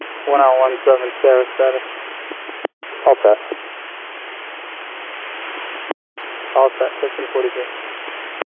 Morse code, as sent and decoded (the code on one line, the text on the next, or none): none